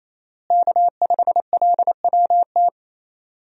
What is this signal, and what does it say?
Morse code, heard as a keyed tone, about 28 wpm, 700 Hz. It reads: K5LWT
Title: USA Call Signs - 5 Characters - Repeated 3x - 28wpm